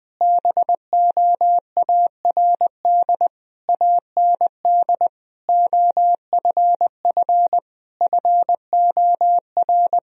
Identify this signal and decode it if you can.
Morse code, heard as a keyed tone, about 20 wpm, 700 Hz. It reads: BOARD AND OFF FOR
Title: Sentences from Top 1000 Words - Review Part 4 - 20wpm